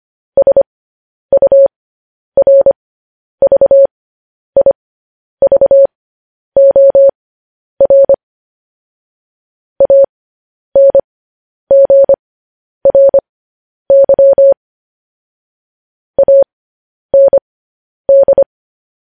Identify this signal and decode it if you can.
Morse code, heard as a keyed tone, about 25 wpm, spaced as for 10 wpm, 550 Hz. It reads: SURVIVOR ANGRY AND
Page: news-headlines-in-morse-code-at-10-wpm